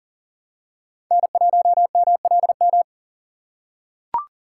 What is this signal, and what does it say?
Morse code, heard as a keyed tone, about 40 wpm, 700 Hz. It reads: N1MLM
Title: USA Call Signs - 3x Character Spacing - 40wpm